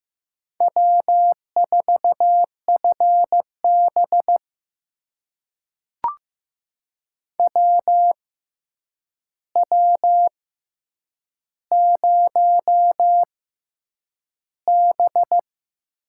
Morse code, heard as a keyed tone, about 15 wpm, spaced as for 8 wpm, 700 Hz. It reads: W4FB WW0B
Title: USA Call Signs - 4 Characters - 6x Character Spacing - 15wpm